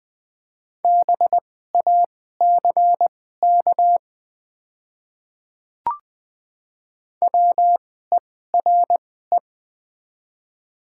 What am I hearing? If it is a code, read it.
Morse code, heard as a keyed tone, about 20 wpm, spaced as for 14 wpm, 700 Hz. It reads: BACK WERE
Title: 2x Character Spacing - Top 200 Words - 20wpm